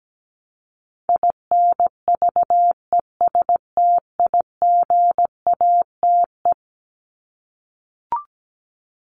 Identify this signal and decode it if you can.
Morse code, heard as a keyed tone, about 17 wpm, 700 Hz. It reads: INVESTIGATE
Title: Top 1751-2000 Words 17wpm